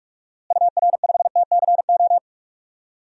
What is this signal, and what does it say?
Morse code, heard as a keyed tone, about 46 wpm, 700 Hz. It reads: UR5TCO
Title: DXCC-100 Call Signs - 6 Characters - Repeated 2x - 45wpm